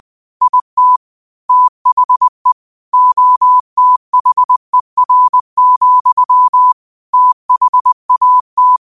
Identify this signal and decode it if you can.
Morse code, heard as a keyed tone, about 20 wpm, 1000 Hz. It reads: IT THE OTHER, THAT
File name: qotdfiles / 20wpm